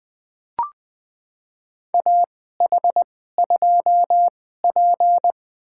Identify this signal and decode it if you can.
Morse code, heard as a keyed tone, about 20 wpm, spaced as for 14 wpm, 700 Hz. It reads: AH2P
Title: USA Call Signs - 2x Character Spacing - 20wpm